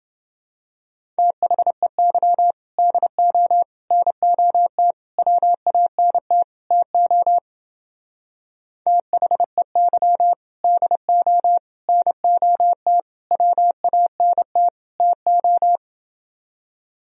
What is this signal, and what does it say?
Morse code, heard as a keyed tone, about 29 wpm, 700 Hz. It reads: THEY DO NOT WANT TO THEY DO NOT WANT TO